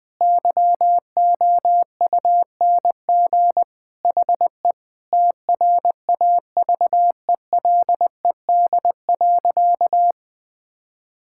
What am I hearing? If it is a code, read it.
Morse code, heard as a keyed tone, about 20 wpm, 700 Hz. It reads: YOUNG HE TRAVELED.
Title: Sentences with 3-Word Phrases - Spoken Beginning - 20wpm